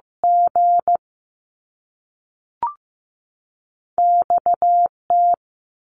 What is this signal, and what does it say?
Morse code, heard as a keyed tone, about 15 wpm, 700 Hz. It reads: G X T
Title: Top Two Letter Combinations in Words 15wpm